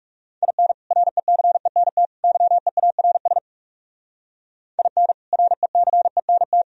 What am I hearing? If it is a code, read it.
Morse code, heard as a keyed tone, about 37 wpm, 700 Hz. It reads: IN RECENT YEARS IN RECENT
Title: Speed-Racing - Sets of 3 Words 30wpm